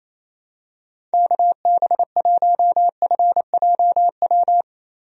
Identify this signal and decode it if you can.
Morse code, heard as a keyed tone, about 28 wpm, 700 Hz. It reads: KB1FJW
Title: USA Call Signs - 6 Characters - 28wpm